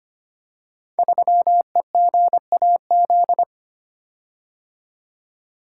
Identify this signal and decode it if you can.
Morse code, heard as a keyed tone, about 25 wpm, 700 Hz. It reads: 3EGAZ
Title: Code Groups - 5 Letter:Numbers - 25wpm